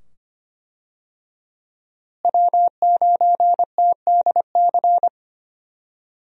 Morse code, heard as a keyed tone, about 25 wpm, 700 Hz. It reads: W9TDC
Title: USA Call Signs - 5 Characters - Repeated 2x - 25wpm